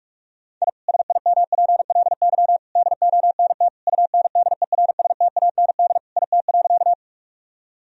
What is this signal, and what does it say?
Morse code, heard as a keyed tone, about 46 wpm, 700 Hz. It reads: I SIMPLY DONT UNDERSTAND IT.
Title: Sentences from Top 1000 Words - Review Part 4 - 45wpm